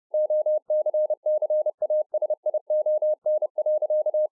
Morse code, heard as a keyed tone, about 30 wpm, 610 Hz. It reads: OCCASION.